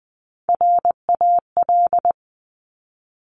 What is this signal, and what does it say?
Morse code, heard as a keyed tone, about 20 wpm, 700 Hz. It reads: RAL
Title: Top Three Letter Combinations in Words 20wpm